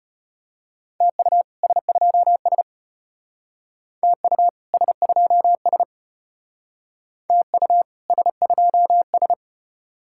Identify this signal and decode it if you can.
Morse code, heard as a keyed tone, about 33 wpm, 700 Hz. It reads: TU S2S TU S2S TU S2S